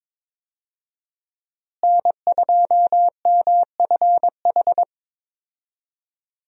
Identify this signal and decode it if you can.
Morse code, heard as a keyed tone, about 22 wpm, 700 Hz. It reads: N2MFH
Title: USA Call Signs - 5 Characters - Repeated 2x - 22wpm